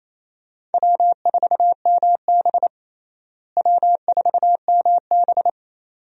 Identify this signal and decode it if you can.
Morse code, heard as a keyed tone, about 28 wpm, 700 Hz. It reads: W4MB W4MB
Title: USA Call Signs - 4 Characters - Repeated 2x - 28wpm